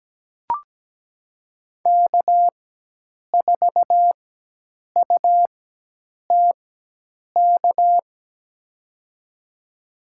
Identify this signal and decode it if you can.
Morse code, heard as a keyed tone, about 17 wpm, spaced as for 8 wpm, 700 Hz. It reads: K4UTK
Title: USA Call Signs - 5 Characters - 4x Character Spacing - 17wpm